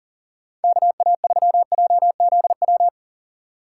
Morse code, heard as a keyed tone, about 38 wpm, 700 Hz. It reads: KA3JZW
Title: USA Call Signs - 6 Characters - 2x Character Spacing - 40wpm